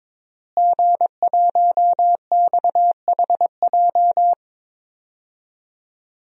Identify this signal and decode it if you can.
Morse code, heard as a keyed tone, about 22 wpm, 700 Hz. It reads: G1XHJ
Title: DXCC-100 Call Signs - 5 Characters - Repeated 3x - 22wpm